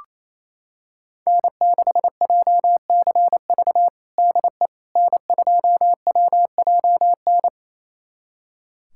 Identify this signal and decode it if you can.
Morse code, heard as a keyed tone, about 28 wpm, 700 Hz. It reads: N6JCV DE N2WJN
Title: USA Callsign DE Callsign - 5 Characters - 28wpm